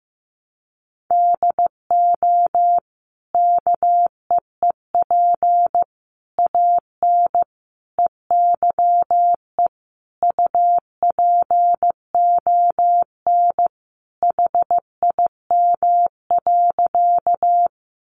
Morse code, heard as a keyed tone, about 15 wpm, 700 Hz. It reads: DO KEEP AN EYE UPON HIM.
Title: Sentences from Top 1100 Words 15wpm